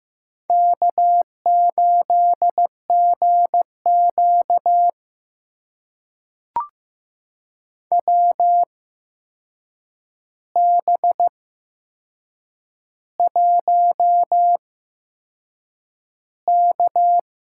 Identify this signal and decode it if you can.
Morse code, heard as a keyed tone, about 15 wpm, spaced as for 6 wpm, 700 Hz. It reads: K8GQ WB1K
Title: USA Call Signs - 4 Characters - 8x Character Spacing - 15wpm